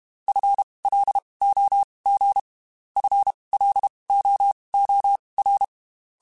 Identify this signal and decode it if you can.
Morse code, heard as a keyed tone, about 32 wpm, spaced as for 23 wpm, 800 Hz. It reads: FLOG FLOOR
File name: Spaced / FL Runs sp mp3s / FL Runs sp @30